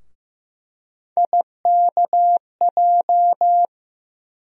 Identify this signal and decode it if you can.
Morse code, heard as a keyed tone, about 15 wpm, 700 Hz. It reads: IKJ
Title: USA Call Sign Postfixes - 3 Letters 15wpm